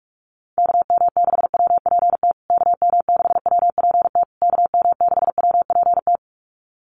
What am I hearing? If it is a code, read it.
Morse code, heard as a keyed tone, about 44 wpm, 700 Hz. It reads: KM6WPT KM6WPT KM6WPT
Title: USA Call Signs Repeated 3x 45wpm